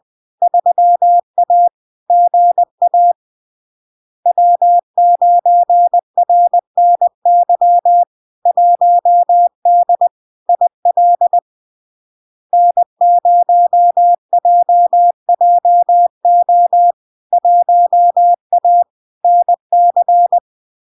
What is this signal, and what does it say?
Morse code, heard as a keyed tone, about 20 wpm, 700 Hz. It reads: 3A GA W9RNY 1D IL N0JJO 1A NC
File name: LARK-FD-2022-Log-in-CW-20wpm-700-Hz